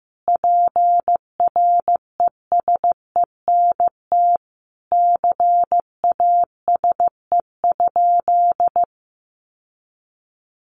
Morse code, heard as a keyed tone, about 14 wpm, 700 Hz. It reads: PRESENT CASE?
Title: Sentences from Top 1200 Words 15wpm